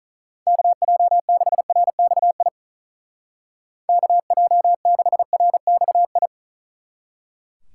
Morse code, heard as a keyed tone, about 45 wpm, 700 Hz. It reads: KJ6RXI KJ6RXI